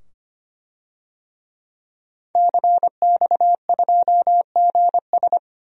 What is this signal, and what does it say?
Morse code, heard as a keyed tone, about 25 wpm, 700 Hz. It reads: CX2GS
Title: DXCC-100 Call Signs - 5 Characters - 25wpm